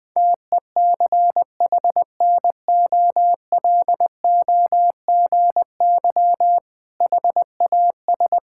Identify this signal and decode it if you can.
Morse code, heard as a keyed tone, about 20 wpm, 700 Hz. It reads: TECHNOLOGY HAS
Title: Sentences with 6-Word Phrases - 20wpm